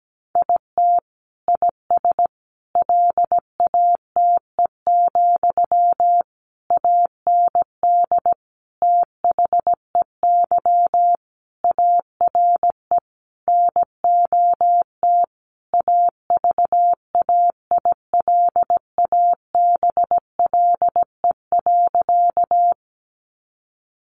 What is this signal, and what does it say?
Morse code, heard as a keyed tone, about 17 wpm, 700 Hz. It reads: IT IS LATE, AND THEY ARE NOT AVAILABLE.
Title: Sentences from Top 600 Words 17wpm